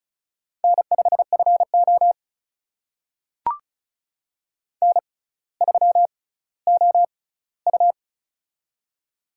Morse code, heard as a keyed tone, about 36 wpm, spaced as for 18 wpm, 700 Hz. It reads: N5FO N3OU
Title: USA Call Signs - 4 Characters - 6x Character Spacing - 35wpm